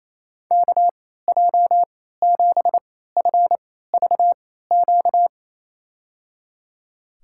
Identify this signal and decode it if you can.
Morse code, heard as a keyed tone, about 27 wpm, spaced as for 16 wpm, 700 Hz. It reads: KJ7FVQ
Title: USA Call Signs - 6 Characters - 3x Character Spacing - 28wpm